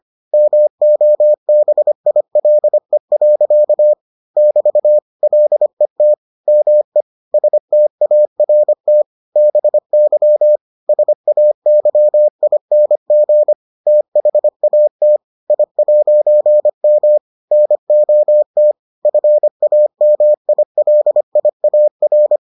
Morse code, heard as a keyed tone, about 25 wpm, 600 Hz. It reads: MOBILE. <BT> LET ME START BY SAYING THAT I'M NOT FAMILIAR